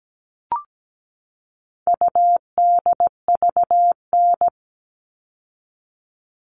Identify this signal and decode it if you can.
Morse code, heard as a keyed tone, about 17 wpm, 700 Hz. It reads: UDVN
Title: All Cut Numbers - 4-Digit 17wpm